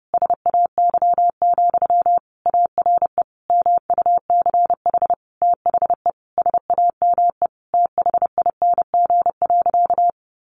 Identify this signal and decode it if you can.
Morse code, heard as a keyed tone, about 30 wpm, 700 Hz. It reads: SAY, ARE MUCH THE SAME THING.
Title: Sentences from Top 1100 Words 30wpm